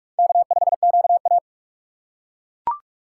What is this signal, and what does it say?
Morse code, heard as a keyed tone, about 42 wpm, 700 Hz. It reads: K5QA
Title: USA Call Signs - 4 Characters - 6x Character Spacing - 45wpm